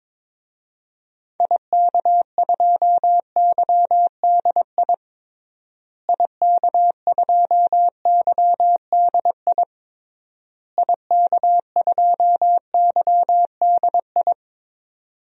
Morse code, heard as a keyed tone, about 22 wpm, 700 Hz. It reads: IK2YDI IK2YDI IK2YDI
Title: DXCC-100 Call Signs - 6 Characters - Repeated 3x - 22wpm